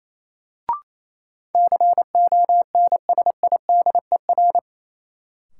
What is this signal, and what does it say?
Morse code, heard as a keyed tone, about 28 wpm, 700 Hz. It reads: CONSIDER
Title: Sentences from Top 600 Words - New Word Drill - 28wpm